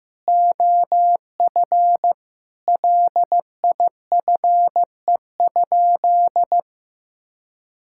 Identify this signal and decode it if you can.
Morse code, heard as a keyed tone, about 15 wpm, 700 Hz. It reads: OF LIFE?
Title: Sentences from Top 1000 Words - Review Part 5 - 15wpm